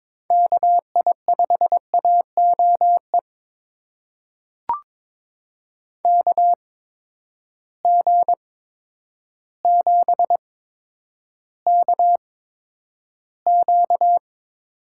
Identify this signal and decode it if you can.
Morse code, heard as a keyed tone, about 21 wpm, spaced as for 10 wpm, 700 Hz. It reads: KI5AOE KG7KQ
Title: USA Call Signs - 8x Character Spacing - 22wpm